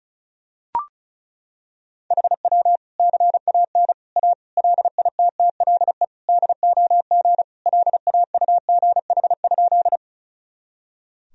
Morse code, heard as a keyed tone, about 34 wpm, 700 Hz. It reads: HW CAN A LITTLE DOG LAUGH?
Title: Sentences from Top 500 Words and Abbreviations 35wpm